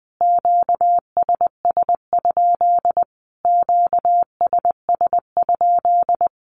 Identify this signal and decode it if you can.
Morse code, heard as a keyed tone, about 20 wpm, 700 Hz. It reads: QSS? QSS?
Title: Q-Codes Repeated 3x 20wpm